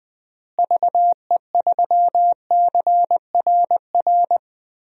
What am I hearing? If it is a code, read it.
Morse code, heard as a keyed tone, about 20 wpm, 700 Hz. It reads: VE3CRR
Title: DXCC-100 Call Signs - 6 Characters - Repeated 3x - 20wpm